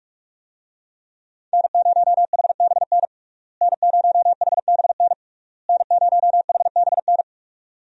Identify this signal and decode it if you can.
Morse code, heard as a keyed tone, about 46 wpm, 700 Hz. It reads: N0HBN N0HBN N0HBN
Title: USA Call Signs - 5 Characters - Repeated 3x - 45wpm